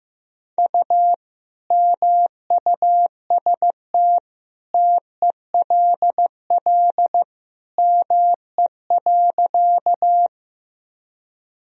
Morse code, heard as a keyed tone, about 16 wpm, 700 Hz. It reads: U MUST TELL ME.
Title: Sentences from Top 200 Words and Abbreviations 15wpm